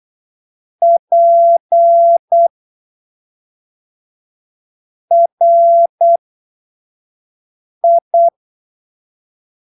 Morse code, heard as a keyed tone, about 8 wpm, spaced as for 3 wpm, 680 Hz. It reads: PRI
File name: arquivos / texto / texto 29 08ppm